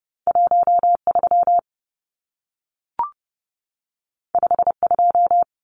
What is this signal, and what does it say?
Morse code, heard as a keyed tone, about 30 wpm, 700 Hz. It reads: 13 52
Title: Numbers - 2-Digit 30wpm